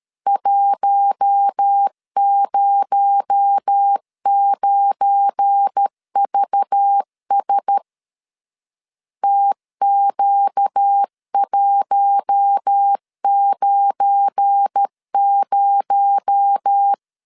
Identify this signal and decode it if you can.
Morse code, heard as a keyed tone, about 13 wpm, 800 Hz. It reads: 109VS TQ190
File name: LEZIONI / 7 GF9Z / 7 GF9 12wpm 2319